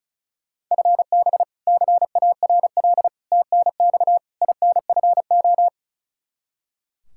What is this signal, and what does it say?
Morse code, heard as a keyed tone, about 36 wpm, 700 Hz. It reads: FB CARL TNX INFO
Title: Practice QSO - Exchange 3 - More Info - 35wpm